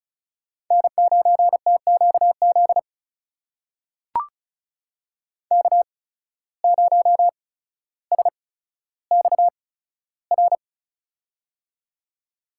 Morse code, heard as a keyed tone, about 36 wpm, spaced as for 15 wpm, 700 Hz. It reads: N9TQZ K0SXR
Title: USA Call Signs - 5 Characters - 8x Character Spacing - 35wpm